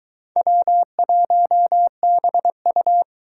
Morse code, heard as a keyed tone, about 23 wpm, 700 Hz. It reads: W1BU